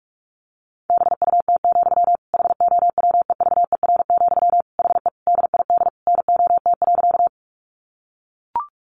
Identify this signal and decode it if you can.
Morse code, heard as a keyed tone, about 44 wpm, 700 Hz. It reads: BUT, HOWEVER, HE DID NOT.
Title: Sentences from Top 600 Words 45wpm